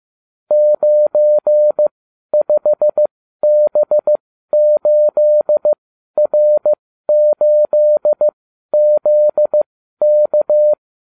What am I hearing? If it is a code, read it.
Morse code, heard as a keyed tone, about 15 wpm, spaced as for 12 wpm, 600 Hz. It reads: 95B8R8ZK